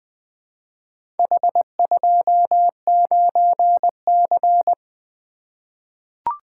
Morse code, heard as a keyed tone, about 20 wpm, 700 Hz. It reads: H29C
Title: Four Letter-Number Combinations - 20wpm